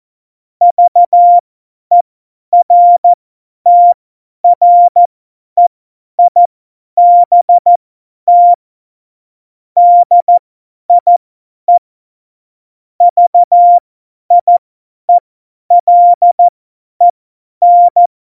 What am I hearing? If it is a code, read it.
Morse code, heard as a keyed tone, about 14 wpm, spaced as for 10 wpm, 700 Hz. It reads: VERTREIBT DIE VIELEN
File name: Morsekurs / Witze / Witz 09. Januar 2018, 50 F70